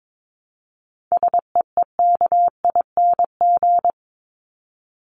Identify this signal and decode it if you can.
Morse code, heard as a keyed tone, about 22 wpm, 700 Hz. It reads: SEEKING
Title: ING Suffix - Top 100 Words with Base Words 22wpm